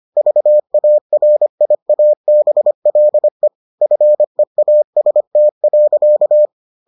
Morse code, heard as a keyed tone, about 25 wpm, 600 Hz. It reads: VARIABLE FEAST.